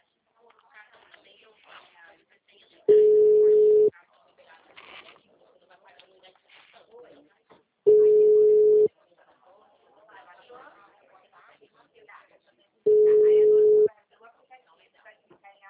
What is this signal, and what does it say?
Ringback tone (Europe, 425 Hz), with speech.